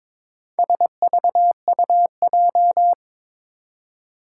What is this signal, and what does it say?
Morse code, heard as a keyed tone, about 22 wpm, 700 Hz. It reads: SVUJ
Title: Four Challenging Letter-Number Combinations 22wpm